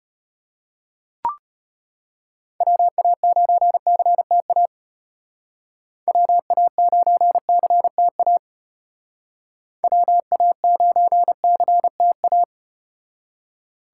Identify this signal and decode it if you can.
Morse code, heard as a keyed tone, about 35 wpm, 700 Hz. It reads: WA9CTA WA9CTA WA9CTA